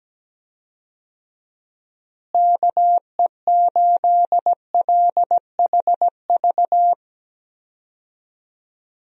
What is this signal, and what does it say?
Morse code, heard as a keyed tone, about 17 wpm, 700 Hz. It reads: KE8LHV
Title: USA Call Signs - 6 Characters - Repeated 3x - 17wpm